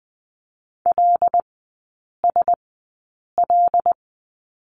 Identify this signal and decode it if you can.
Morse code, heard as a keyed tone, about 19 wpm, spaced as for 8 wpm, 700 Hz. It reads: LSL
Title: 074 Lesson 13e - Letter L - Three Characters 20wpm